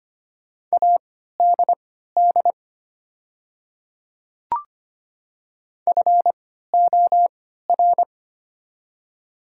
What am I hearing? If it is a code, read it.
Morse code, heard as a keyed tone, about 25 wpm, spaced as for 14 wpm, 700 Hz. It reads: ADD FOR
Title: 3x Character Spacing - Top 300 Words - 25wpm